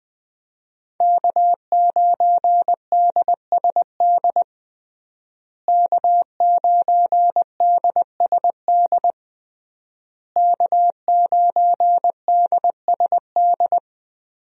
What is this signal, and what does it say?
Morse code, heard as a keyed tone, about 20 wpm, 700 Hz. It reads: K9DSD K9DSD K9DSD
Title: USA Call Signs - 5 Characters - Repeated 3x - 20wpm